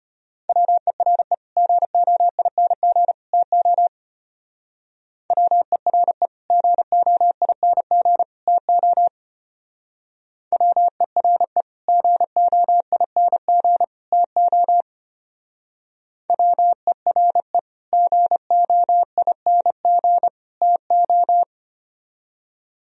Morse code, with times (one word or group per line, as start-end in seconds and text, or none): 0.5-1.3 WERE
1.6-3.1 GOING
3.3-3.9 TO
5.3-6.3 WERE
6.5-8.2 GOING
8.5-9.1 TO
10.5-11.6 WERE
11.9-13.8 GOING
14.1-14.8 TO
16.3-17.6 WERE
17.9-20.3 GOING
20.6-21.4 TO